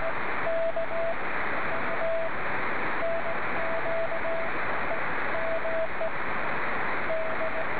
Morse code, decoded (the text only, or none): none